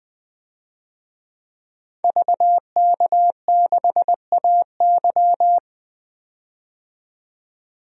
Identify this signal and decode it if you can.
Morse code, heard as a keyed tone, about 20 wpm, 700 Hz. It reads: VK6AY
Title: DXCC-100 Call Signs - 5 Characters - 20wpm